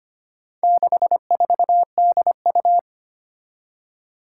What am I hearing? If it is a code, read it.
Morse code, heard as a keyed tone, about 25 wpm, 700 Hz. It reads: 64DU
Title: Four Challenging Letter-Number Combinations 25wpm